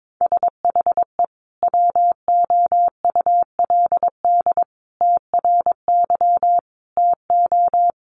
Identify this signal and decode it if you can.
Morse code, heard as a keyed tone, about 22 wpm, 700 Hz. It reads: SHE WOULD TRY TO
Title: Sentences from Top 600 Words 22wpm